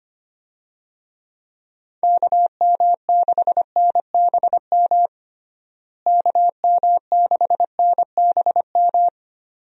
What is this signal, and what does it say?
Morse code, heard as a keyed tone, about 25 wpm, 700 Hz. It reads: KM6NBM KM6NBM
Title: USA Call Signs - 6 Characters - Repeated 2x - 25wpm